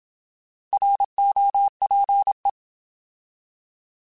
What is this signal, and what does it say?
Morse code, heard as a keyed tone, about 27 wpm, 800 Hz. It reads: ROPE
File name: Alphabetical / Alphabetical mp3s / Rs @25